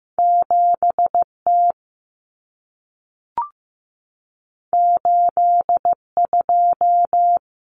Morse code, heard as a keyed tone, about 15 wpm, 700 Hz. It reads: 7 T 8 2
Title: Common Cut Numbers - 2-Digit 15wpm